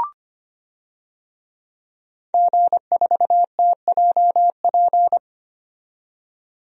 Morse code, heard as a keyed tone, about 25 wpm, 690 Hz. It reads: G4TJP
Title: DXCC-100 Call Signs - 5 Characters - 25wpm